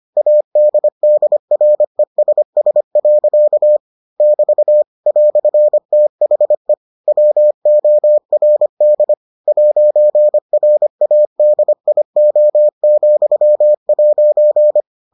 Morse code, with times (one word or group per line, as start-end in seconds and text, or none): 0.2-3.8 ADDRESS.
4.2-4.8 <BT>
5.1-6.7 "THE
7.1-9.1 WORD
9.5-14.8 'RADIO,'